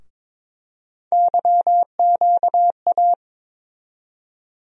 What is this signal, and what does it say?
Morse code, heard as a keyed tone, about 22 wpm, 700 Hz. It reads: YQA
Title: USA Call Sign Postfixes - 3 Letters 22wpm